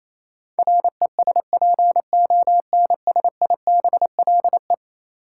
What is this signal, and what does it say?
Morse code, heard as a keyed tone, about 28 wpm, 700 Hz. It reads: RESPONSIBLE
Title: Sentences from Top 1000 Words - Review Part 7 - New Word Drill - 28wpm